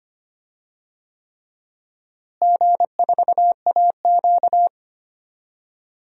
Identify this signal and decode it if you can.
Morse code, heard as a keyed tone, about 25 wpm, 700 Hz. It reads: G4AQ
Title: DXCC-100 Call Signs - 4 Characters - Repeated 3x - 25wpm